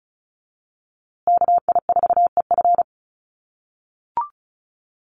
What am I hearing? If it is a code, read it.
Morse code, heard as a keyed tone, about 34 wpm, 700 Hz. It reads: KI4EF
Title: USA Call Signs Repeated 2x 35wpm